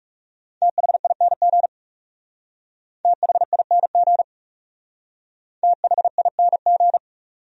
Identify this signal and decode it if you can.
Morse code, heard as a keyed tone, about 38 wpm, 700 Hz. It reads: THING THING THING